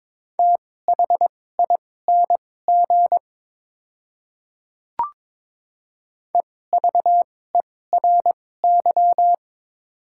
Morse code, heard as a keyed tone, about 22 wpm, spaced as for 16 wpm, 690 Hz. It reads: THING EVERY
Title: 2x Character Spacing - Top 200 Words - 22wpm